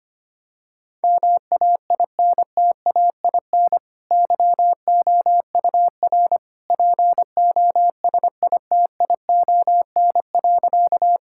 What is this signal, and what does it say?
Morse code, heard as a keyed tone, about 25 wpm, 700 Hz. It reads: MAINTAIN YOUR POSITION.
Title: Sentences from Top 1000 Words - Review Part 5 - 25wpm